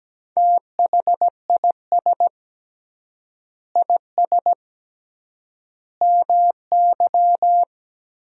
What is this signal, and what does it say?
Morse code, heard as a keyed tone, about 17 wpm, 700 Hz. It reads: THIS IS MY
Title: 3x Word-Spacing - Sentences from Top 300 Words - 17wpm